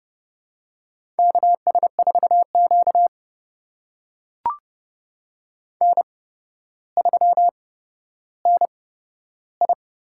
Morse code, heard as a keyed tone, about 30 wpm, spaced as for 13 wpm, 700 Hz. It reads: KS4Q N3NI
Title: USA Call Signs - 4 Characters - 8x Character Spacing - 30wpm